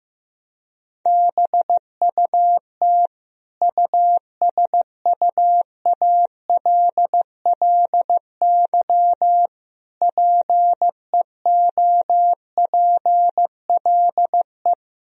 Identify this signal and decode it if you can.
Morse code, heard as a keyed tone, about 15 wpm, 700 Hz. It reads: BUT USUALLY PEOPLE
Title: Sentences from Top 1000 Words - Review Part 3 - 15wpm